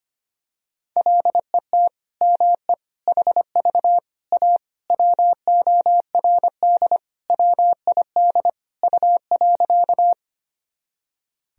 Morse code, heard as a keyed tone, about 25 wpm, 700 Hz. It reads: LET ME HV A WORD WID U.
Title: Sentences from Top 300 Words and Abbreviations 25wpm